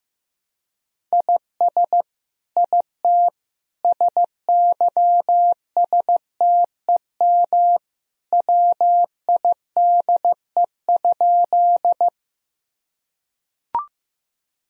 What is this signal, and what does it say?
Morse code, heard as a keyed tone, about 15 wpm, 700 Hz. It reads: IS IT SYSTEM WIDE?
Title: Sentences from Top 1000 Words - Review Part 3 - 15wpm